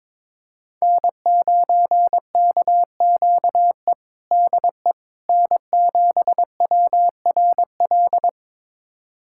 Morse code, heard as a keyed tone, about 21 wpm, 700 Hz. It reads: N9KQE DE N7WRL
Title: USA Callsign DE Callsign - 5 Characters - 22wpm